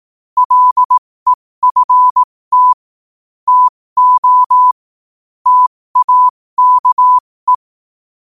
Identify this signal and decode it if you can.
Morse code, heard as a keyed tone, about 18 wpm, spaced as for 15 wpm, 1000 Hz. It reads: LEFT TO TAKE
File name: qotdfiles / 15wpm